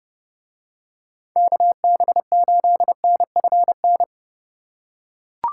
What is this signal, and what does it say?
Morse code, heard as a keyed tone, about 30 wpm, 700 Hz. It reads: KB8NFN